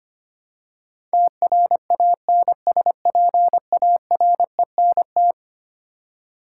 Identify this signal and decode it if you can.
Morse code, heard as a keyed tone, about 25 wpm, 700 Hz. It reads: TRANSPARENT
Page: morse-code-ninja-practice-25wpm